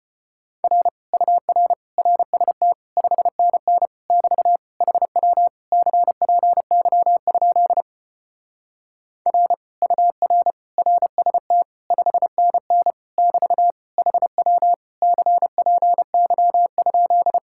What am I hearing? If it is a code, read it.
Morse code, heard as a keyed tone, about 33 wpm, 700 Hz. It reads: R UR RST 5NN <BT> HW CPY? R UR RST 5NN <BT> HW CPY?